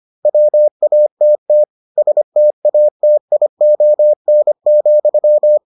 Morse code, heard as a keyed tone, about 25 wpm, 600 Hz. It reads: WATT STATION,